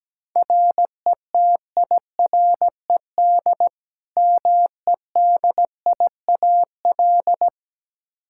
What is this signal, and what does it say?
Morse code, heard as a keyed tone, about 17 wpm, 700 Hz. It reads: RETIRED MEDIAL
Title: Practice QSO - Exchange 3 - More Info - 17wpm